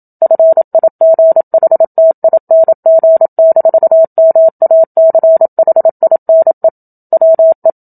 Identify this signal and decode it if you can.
Morse code, heard as a keyed tone, about 27 wpm, 650 Hz. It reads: FIGHTING-MACHINE WE
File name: Book Two / 01-Chapter 1 - 27 WPM